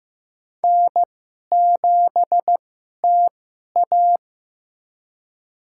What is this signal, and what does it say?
Morse code, heard as a keyed tone, about 15 wpm, spaced as for 11 wpm, 700 Hz. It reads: N7TA
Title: USA Call Signs - 2x Character Spacing - 15wpm